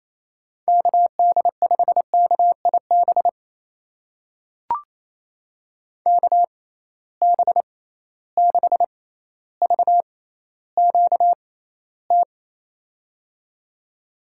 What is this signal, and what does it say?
Morse code, heard as a keyed tone, about 28 wpm, spaced as for 14 wpm, 700 Hz. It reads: KD5KIB KB6VQT
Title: USA Call Signs - 6 Characters - 6x Character Spacing - 28wpm